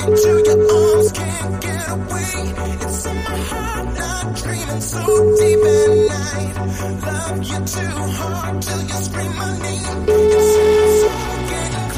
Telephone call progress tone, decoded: ringback tone

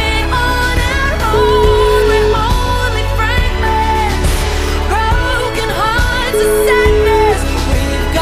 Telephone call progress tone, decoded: ringback tone

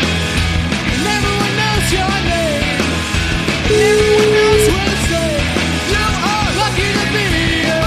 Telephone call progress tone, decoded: ringback tone

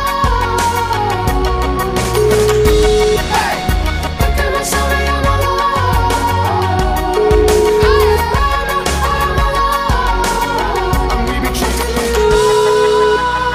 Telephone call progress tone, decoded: ringback tone

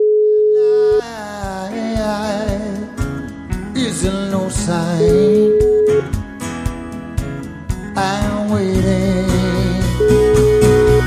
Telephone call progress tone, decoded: ringback tone

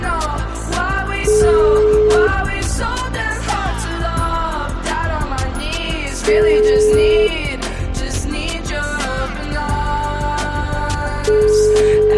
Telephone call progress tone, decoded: ringback tone

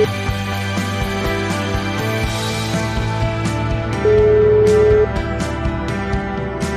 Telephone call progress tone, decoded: ringback tone

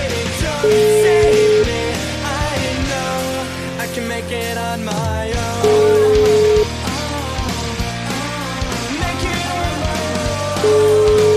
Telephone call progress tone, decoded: ringback tone

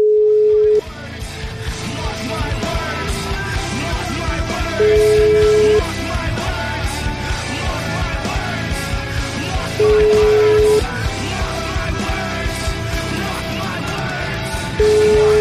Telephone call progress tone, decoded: ringback tone

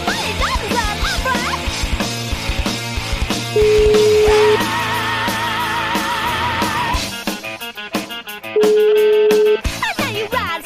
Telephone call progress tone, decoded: ringback tone

